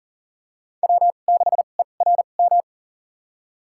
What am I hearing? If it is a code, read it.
Morse code, heard as a keyed tone, about 40 wpm, spaced as for 29 wpm, 700 Hz. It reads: W6ERM
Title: USA Call Signs - 5 Characters - 2x Character Spacing - 40wpm